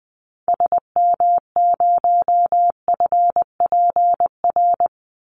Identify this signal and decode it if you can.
Morse code, heard as a keyed tone, about 20 wpm, 700 Hz. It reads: SM0FPR
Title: Call Signs Repeated 2x - 20wpm